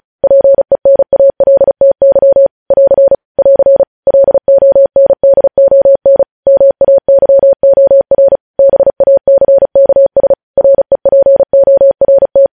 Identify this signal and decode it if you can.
Morse code, heard as a keyed tone, about 36 wpm, 550 Hz. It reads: PENALTY <AR> <AR> LONDON MAYOR BACKS REPORT
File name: podcast / News headlines-1x-35x35